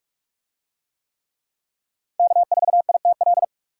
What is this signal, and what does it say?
Morse code, heard as a keyed tone, about 44 wpm, 700 Hz. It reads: K4ITL